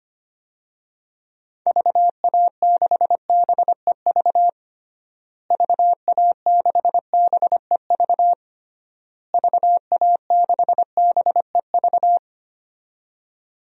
Morse code, heard as a keyed tone, about 25 wpm, 700 Hz. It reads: VA6BEV VA6BEV VA6BEV
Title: DXCC-100 Call Signs - 6 Characters - Repeated 3x - 25wpm